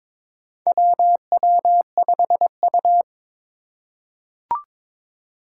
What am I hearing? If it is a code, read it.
Morse code, heard as a keyed tone, about 22 wpm, 700 Hz. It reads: WW5U
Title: USA Call Signs - 4 Characters - 2x Character Spacing - 22wpm